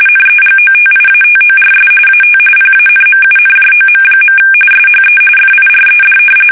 RTTY, about 45.45 baud. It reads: NEGOTIATORS AT PANMUNJOM. SHE FIRED
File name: fsk850 / battleshipnj